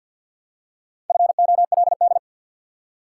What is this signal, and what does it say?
Morse code, heard as a keyed tone, about 53 wpm, 700 Hz. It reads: FOLD
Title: Four Letter Words - 50wpm